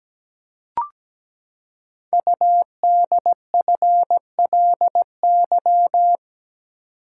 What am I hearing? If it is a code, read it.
Morse code, heard as a keyed tone, about 17 wpm, 700 Hz. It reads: UDFLY
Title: Code Groups - 5 Letters - 17wpm